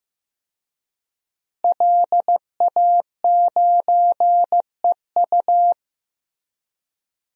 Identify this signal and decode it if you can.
Morse code, heard as a keyed tone, about 15 wpm, 700 Hz. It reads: LA9EU
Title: DXCC-100 Call Signs - 5 Characters - Repeated 2x - 15wpm